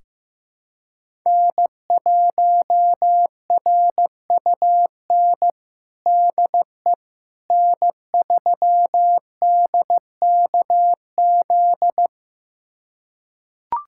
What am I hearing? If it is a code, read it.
Morse code, heard as a keyed tone, about 15 wpm, 700 Hz. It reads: N1RUN DE N3DKZ
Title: USA Callsign DE Callsign - 5 Characters - 15wpm